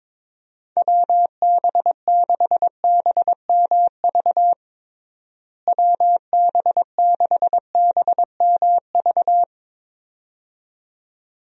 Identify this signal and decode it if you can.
Morse code, heard as a keyed tone, about 22 wpm, 700 Hz. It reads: WB6BMV WB6BMV
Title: USA Call Signs - 6 Characters - Repeated 2x - 22wpm